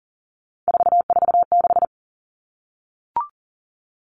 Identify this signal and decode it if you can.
Morse code, heard as a keyed tone, about 40 wpm, 700 Hz. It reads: 446
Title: Numbers - 3-Digit 40wpm